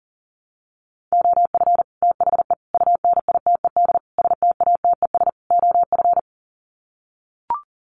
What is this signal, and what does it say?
Morse code, heard as a keyed tone, about 37 wpm, 700 Hz. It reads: OF THE UNITED STATES OF
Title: Sets of 5 Words Encore 40wpm